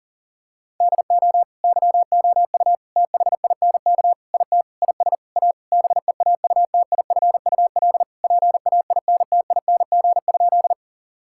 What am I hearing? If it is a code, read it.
Morse code, heard as a keyed tone, about 40 wpm, 700 Hz. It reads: DO YOU THINK IT IS A BEAUTIFUL PAINTING?
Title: Sentences from Top 1000 Words - Review Part 6 - 40wpm